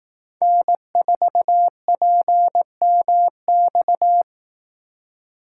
Morse code, heard as a keyed tone, about 18 wpm, 700 Hz. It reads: N4PMX
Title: Speed Racing - USA Call Signs 15wpm